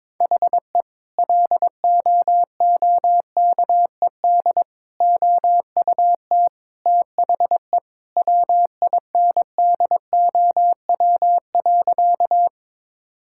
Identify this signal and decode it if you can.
Morse code, heard as a keyed tone, about 22 wpm, 700 Hz. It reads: HE LOOKED OUT THE WINDOW.
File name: practice / 22wpm / Sentences with 2-Word Phrases - 22wpm